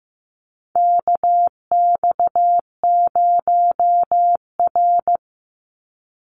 Morse code, heard as a keyed tone, about 15 wpm, 700 Hz. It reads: KX0R
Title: NA SOTA Chaser Callsigns 15wpm